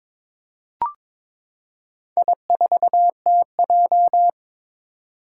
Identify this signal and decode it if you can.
Morse code, heard as a keyed tone, about 22 wpm, 700 Hz. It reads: I4TJ
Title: Four Challenging Letter-Number Combinations 22wpm